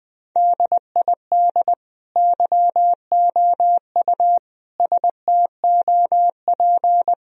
Morse code, heard as a keyed tone, about 20 wpm, 700 Hz. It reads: DID YOU STOP
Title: Sentences from Top 1000 Words - Review Part 1 - 20wpm